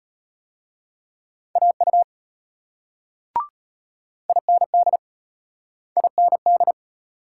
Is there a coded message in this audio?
Yes: Morse code from 1.6 to 6.7 s, heard as a keyed tone, about 33 wpm, 700 Hz. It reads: AU IND IND